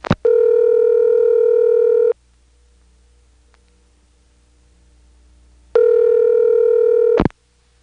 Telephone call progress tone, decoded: ringback tone